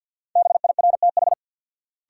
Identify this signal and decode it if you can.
Morse code, heard as a keyed tone, about 50 wpm, 700 Hz. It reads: BIRTH
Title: Sentences from Top 800 Words - New Word Drill - 50wpm